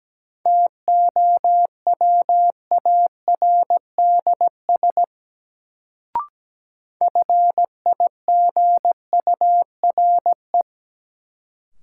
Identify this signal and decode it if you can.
Morse code, heard as a keyed tone, about 17 wpm, 700 Hz. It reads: TOWARDS FIGURE
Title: Sentences from Top 1000 Words - Review Part 2 - New Word Drill - 17wpm